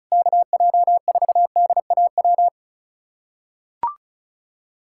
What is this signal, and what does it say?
Morse code, heard as a keyed tone, about 36 wpm, 700 Hz. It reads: KJ4DAW
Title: USA Callsign DE Callsign - 6 Characters - 35wpm